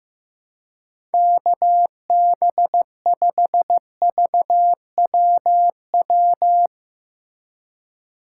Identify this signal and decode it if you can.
Morse code, heard as a keyed tone, about 15 wpm, 700 Hz. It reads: KB5VWW
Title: USA Call Signs - 6 Characters - 2x Character Spacing - 15wpm